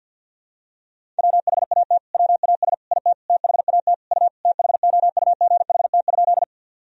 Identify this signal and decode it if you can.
Morse code, heard as a keyed tone, about 47 wpm, 700 Hz. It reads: WHAT WAS IT THAT U THOUGHT?
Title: Sentences from Top 300 Words and Abbreviations 50wpm